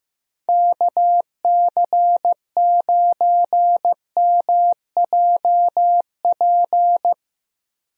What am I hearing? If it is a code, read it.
Morse code, heard as a keyed tone, about 15 wpm, 700 Hz. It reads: KC9MJP
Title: USA Call Signs - 6 Characters - Repeated 2x - 15wpm